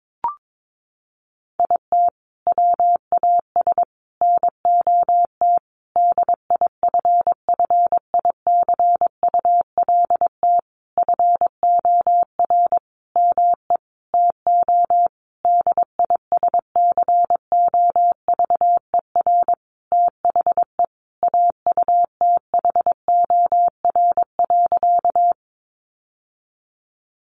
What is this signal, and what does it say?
Morse code, heard as a keyed tone, about 22 wpm, 700 Hz. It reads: IT WAS NOT DIFFICULT FOR ME TO DISCOVER THE AUTHOR.
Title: Sentences from Top 1200 Words 22wpm